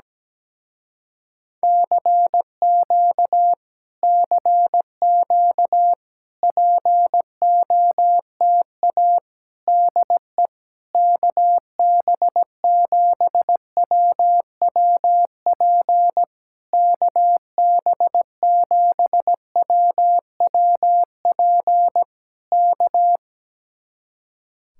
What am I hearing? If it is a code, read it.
Morse code, heard as a keyed tone, about 17 wpm, 700 Hz. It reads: CQ CQ POTA DE KB7WWP KB7WWP K
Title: Speed-Racing Common QSO Phrases - 17wpm